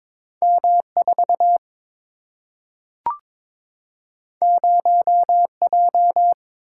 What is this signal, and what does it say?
Morse code, heard as a keyed tone, about 21 wpm, 700 Hz. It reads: M 4 0 J
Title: Single-Letter Single-Number 22wpm